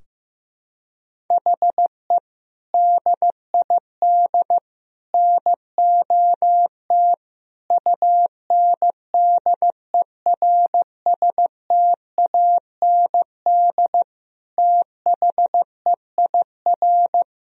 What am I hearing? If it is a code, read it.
Morse code, heard as a keyed tone, about 15 wpm, 700 Hz. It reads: HE DID NOT UNDERSTAND THEIR
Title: Sentences from Top 1000 Words - Review Part 3 - 15wpm